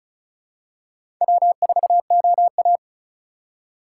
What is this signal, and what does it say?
Morse code, heard as a keyed tone, about 34 wpm, 700 Hz. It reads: W4OA
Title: USA Call Signs - 2x Character Spacing - 35wpm